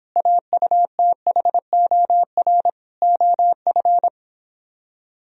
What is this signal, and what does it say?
Morse code, heard as a keyed tone, about 26 wpm, 700 Hz. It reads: AUTHOR OF